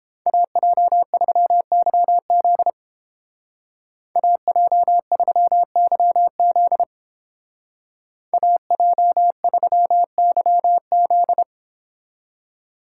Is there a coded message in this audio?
Yes: Morse code from 0.3 to 11.4 s, heard as a keyed tone, about 30 wpm, 700 Hz. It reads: AJ3YZ AJ3YZ AJ3YZ